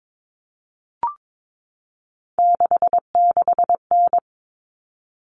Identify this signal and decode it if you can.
Morse code, heard as a keyed tone, about 22 wpm, 700 Hz. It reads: 66N
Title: All Cut Numbers - 3-Digit 22wpm